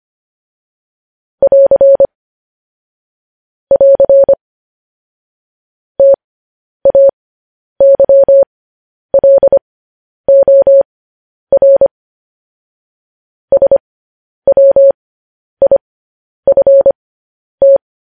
Morse code, heard as a keyed tone, about 25 wpm, spaced as for 10 wpm, 550 Hz. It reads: <AR> <AR> TAYLOR SWIFT
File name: podcast / News headlines-1x-25x10